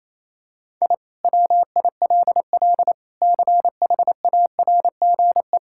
Morse code, heard as a keyed tone, about 28 wpm, 700 Hz. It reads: I WILL CHARGE
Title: Sentences from Top 1000 Words - Review Part 6 - 28wpm